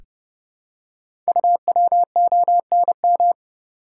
Morse code, heard as a keyed tone, about 30 wpm, 700 Hz. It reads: UWONM
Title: Code Groups - 5 Letters - 30wpm